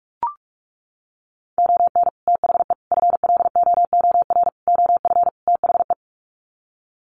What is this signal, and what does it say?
Morse code, heard as a keyed tone, about 44 wpm, 700 Hz. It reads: ON THE FLOOR OF THE
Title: Sets of 5 Words 45wpm